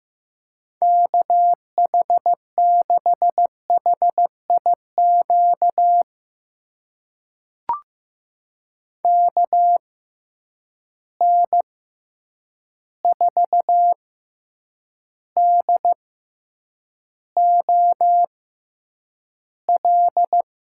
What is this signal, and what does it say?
Morse code, heard as a keyed tone, about 15 wpm, spaced as for 8 wpm, 700 Hz. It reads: KH6HIQ KN4DOL
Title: USA Call Signs - 6 Characters - 6x Character Spacing - 15wpm